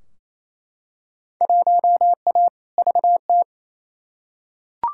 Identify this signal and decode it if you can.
Morse code, heard as a keyed tone, about 28 wpm, 700 Hz. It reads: 1A VT
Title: Field Day Exchanges Encore - 28wpm